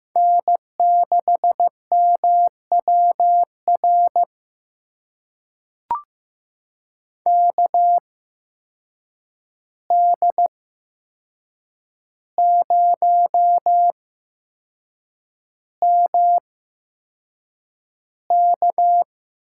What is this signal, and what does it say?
Morse code, heard as a keyed tone, about 15 wpm, spaced as for 6 wpm, 700 Hz. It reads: N6MWR KD0MK
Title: USA Call Signs - 5 Characters - 8x Character Spacing - 15wpm